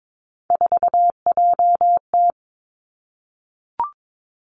Morse code, heard as a keyed tone, about 22 wpm, 700 Hz. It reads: 4JT
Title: Three Challenging Letter-Number Combinations 22wpm